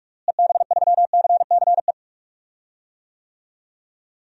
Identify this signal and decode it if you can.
Morse code, heard as a keyed tone, about 45 wpm, 700 Hz. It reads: EB3CCE
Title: DXCC-100 Call Signs - 6 Characters - Repeated 3x - 45wpm